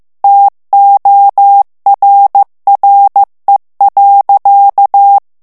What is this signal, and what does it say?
Morse code, heard as a keyed tone, about 15 wpm, 800 Hz. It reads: TORRE.
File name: ITALIEN texte clair 1 vitesse 14 mots minute